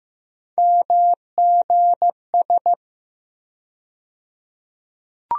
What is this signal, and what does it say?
Morse code, heard as a keyed tone, about 15 wpm, 700 Hz. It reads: MGS
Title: USA Call Sign Postfixes - 3 Letters 15wpm